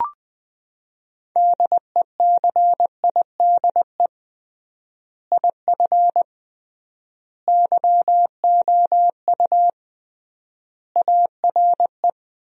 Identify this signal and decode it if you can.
Morse code, heard as a keyed tone, about 20 wpm, 700 Hz. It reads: DECIDE IF YOU ARE
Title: 3x Word-Spacing - Sentences from Top 500 Words - 20wpm